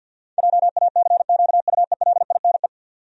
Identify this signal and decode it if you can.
Morse code, heard as a keyed tone, about 52 wpm, 700 Hz. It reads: JACQUELINE
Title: USA Common Names - 50wpm